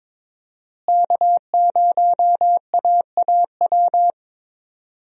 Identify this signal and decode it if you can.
Morse code, heard as a keyed tone, about 22 wpm, 700 Hz. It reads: K0AAW
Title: USA Call Signs - 5 Characters - Repeated 2x - 22wpm